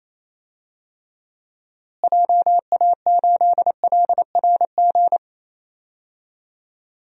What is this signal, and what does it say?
Morse code, heard as a keyed tone, about 27 wpm, 700 Hz. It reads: JA8LRG
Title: DXCC-100 Call Signs - 6 Characters - Repeated 3x - 28wpm